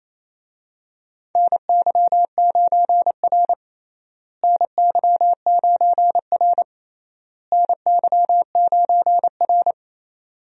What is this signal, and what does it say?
Morse code, heard as a keyed tone, about 28 wpm, 700 Hz. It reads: NY9R NY9R NY9R
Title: USA Call Signs - 4 Characters - Repeated 3x - 28wpm